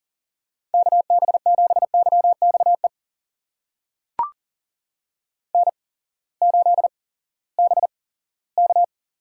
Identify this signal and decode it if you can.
Morse code, heard as a keyed tone, about 40 wpm, spaced as for 20 wpm, 700 Hz. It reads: KB7YXE N8BK
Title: USA Call Signs - 8x Character Spacing - 40wpm